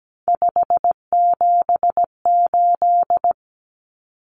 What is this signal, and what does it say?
Morse code, heard as a keyed tone, about 17 wpm, 700 Hz. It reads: 578
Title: Numbers - 3-Digit 17wpm